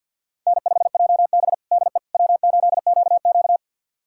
Morse code, heard as a keyed tone, about 49 wpm, 700 Hz. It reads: N5JB DE W8QQ
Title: USA Callsign DE Callsign - 4 Characters - 50wpm